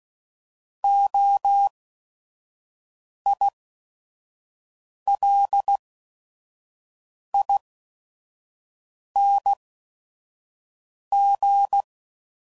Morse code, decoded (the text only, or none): OILING